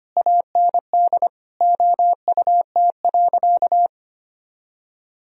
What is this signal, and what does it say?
Morse code, heard as a keyed tone, about 25 wpm, 700 Hz. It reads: AND OUT.
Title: Sentences from Top 1000 Words - Review Part 2 - 25wpm